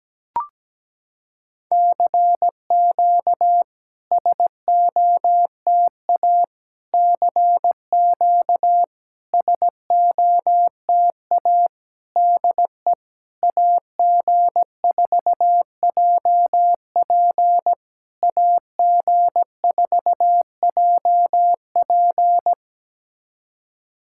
Morse code, heard as a keyed tone, about 17 wpm, 700 Hz. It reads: CQ SOTA CQ SOTA DE AG4JP AG4JP
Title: SOTA QSO Practice 17wpm